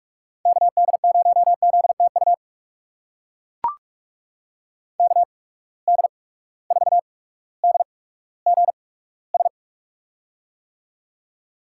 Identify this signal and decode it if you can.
Morse code, heard as a keyed tone, about 42 wpm, spaced as for 19 wpm, 700 Hz. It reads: KD0ZTU KD4DGS
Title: USA Call Signs - 6 Characters - 8x Character Spacing - 45wpm